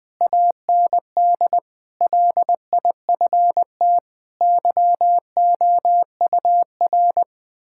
Morse code, heard as a keyed tone, about 20 wpm, 700 Hz. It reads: AND LIFT YOUR
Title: Difficult Sentences with Hints - 20wpm